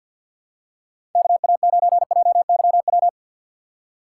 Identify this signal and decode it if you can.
Morse code, heard as a keyed tone, about 50 wpm, 700 Hz. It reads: KA9JYW
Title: USA Call Signs - 4x Character Spacing - 50wpm